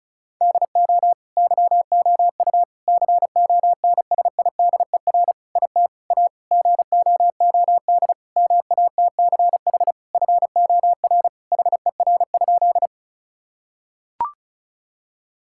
Morse code, heard as a keyed tone, about 36 wpm, 700 Hz. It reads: DO YOU CONSIDER IT A GOOD MATCH FOR HER?
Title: Sentences from Top 1600 Words 35wpm